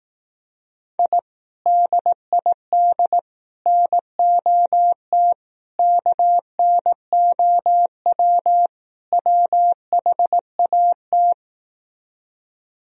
Morse code, heard as a keyed tone, about 18 wpm, 700 Hz. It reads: I DID NOT KNOW WHAT
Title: Speed-Racing - Sets of 5 Words 15wpm